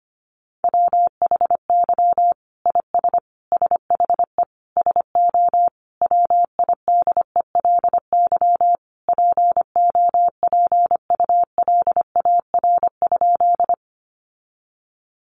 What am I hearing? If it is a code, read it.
Morse code, heard as a keyed tone, about 25 wpm, 700 Hz. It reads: WHY IS SHE SO WIDELY POPULAR?
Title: Sentences from Top 1200 Words 25wpm